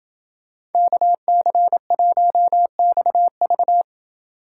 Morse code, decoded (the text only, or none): KC1XV